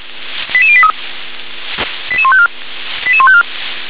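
Five-tone selective call (ZVEI), heard as quick runs of five tones, tones of about 70 ms. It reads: 88083 88154 88154
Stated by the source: ZVEI five-tone selective call received on an SDR.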